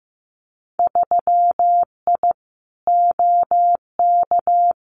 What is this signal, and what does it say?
Morse code, heard as a keyed tone, about 15 wpm, 700 Hz. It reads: 3I OK
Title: Winter Field Day Exchanges Encore 15wpm